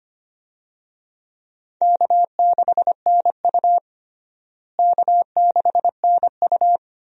Morse code, heard as a keyed tone, about 25 wpm, 700 Hz. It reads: K6NU K6NU
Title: USA Call Signs - 4 Characters - Repeated 2x - 25wpm